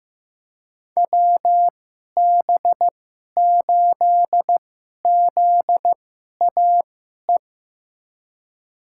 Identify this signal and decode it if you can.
Morse code, heard as a keyed tone, about 15 wpm, spaced as for 11 wpm, 700 Hz. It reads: WB8ZAE
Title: USA Call Signs - 2x Character Spacing - 15wpm